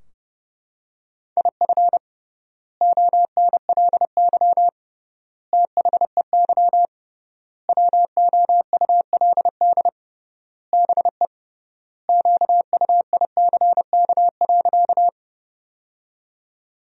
Morse code, heard as a keyed tone, about 30 wpm, 700 Hz. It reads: IF ONLY THEY WOULD BE QUICK.
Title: 3x Word-Spacing - Sentences from Top 500 Words - 30wpm